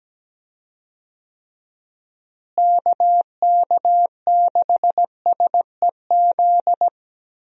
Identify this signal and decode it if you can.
Morse code, heard as a keyed tone, about 17 wpm, 700 Hz. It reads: KK6SEZ
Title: USA Call Signs - 6 Characters - Repeated 2x - 17wpm